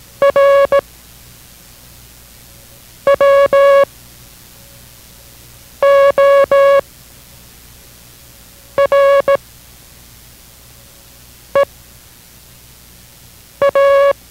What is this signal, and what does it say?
Morse code, heard as a keyed tone, about 17 wpm, spaced as for 4 wpm, 560 Hz. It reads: R WORE A